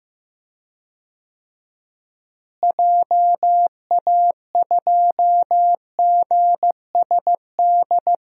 Morse code, heard as a keyed tone, about 15 wpm, 700 Hz. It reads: JA2GSD
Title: DXCC-100 Call Signs - 6 Characters - Repeated 2x - 15wpm